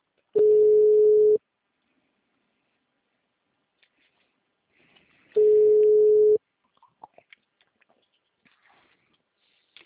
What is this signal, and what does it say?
Ringback tone (Europe, 425 Hz).